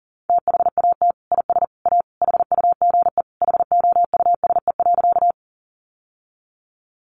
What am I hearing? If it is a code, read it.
Morse code, heard as a keyed tone, about 40 wpm, 700 Hz. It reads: THAT IS A HUGE HOUSE.
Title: Sentences from Top 1100 Words 40wpm